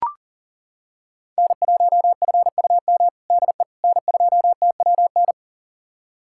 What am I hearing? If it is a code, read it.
Morse code, heard as a keyed tone, about 40 wpm, 700 Hz. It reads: N1FUM DE N2TWN
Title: USA Callsign DE Callsign - 5 Characters - 40wpm